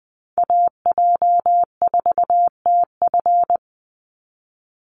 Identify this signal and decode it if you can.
Morse code, heard as a keyed tone, about 20 wpm, 700 Hz. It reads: AJ4TF
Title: USA Call Signs 20wpm